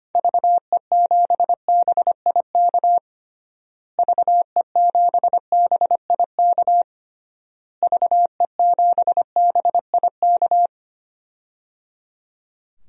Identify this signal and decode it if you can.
Morse code, heard as a keyed tone, about 25 wpm, 700 Hz. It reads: VE7BIK VE7BIK VE7BIK
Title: DXCC-100 Call Signs - 6 Characters - Repeated 3x - 25wpm